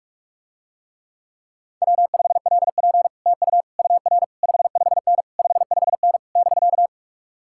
Morse code, heard as a keyed tone, about 49 wpm, 700 Hz. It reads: W5LP TU UR 55N 55N <BK>